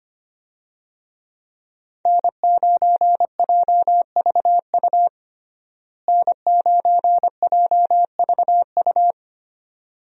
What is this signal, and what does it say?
Morse code, heard as a keyed tone, about 25 wpm, 700 Hz. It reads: N9JVU N9JVU
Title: USA Call Signs - 5 Characters - Repeated 2x - 25wpm